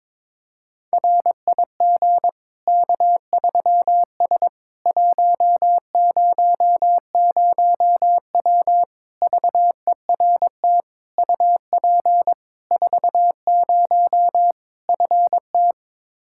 Morse code, heard as a keyed tone, about 22 wpm, 700 Hz. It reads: RIG K3S 100W VERT UP 40 FT
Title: K1USN SST QSO Practice 22wpm